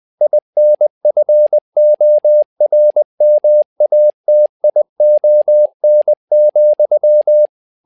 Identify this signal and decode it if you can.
Morse code, heard as a keyed tone, about 20 wpm, 600 Hz. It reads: INFORMATION,
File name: learning center / tower trouble / tower trouble 20wpm